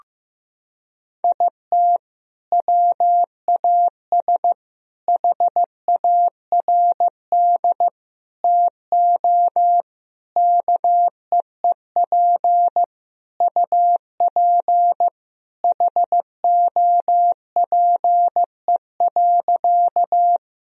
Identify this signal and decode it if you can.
Morse code, heard as a keyed tone, about 15 wpm, 700 Hz. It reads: IT WAS HARD TO KEEP UP HOPE.
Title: Sentences from Top 1000 Words - Review Part 1 - 15wpm